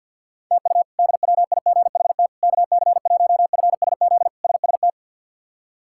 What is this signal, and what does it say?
Morse code, heard as a keyed tone, about 52 wpm, 700 Hz. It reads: TU DWIGHT KC1FSZ SST
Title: K1USN SST QSO Practice - 50wpm